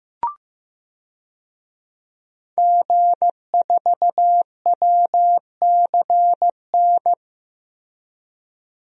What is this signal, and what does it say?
Morse code, heard as a keyed tone, about 15 wpm, 700 Hz. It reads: G4WCN
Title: DXCC-100 Call Signs - 5 Characters - Repeated 2x - 15wpm